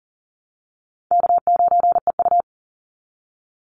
Morse code, heard as a keyed tone, about 40 wpm, 700 Hz. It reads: K9EU
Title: Easy Call Signs Repeated 2x - 40wpm